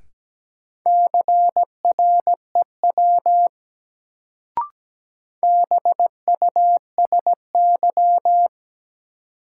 Morse code, heard as a keyed tone, about 17 wpm, 700 Hz. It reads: CREW BUSY
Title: Sentences from Top 1500 Words - New Word Drill Alt - 17wpm